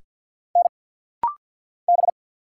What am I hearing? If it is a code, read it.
Morse code, heard as a keyed tone, about 50 wpm, 700 Hz. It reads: N B
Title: Single Letters - Rapid-Fire 50wpm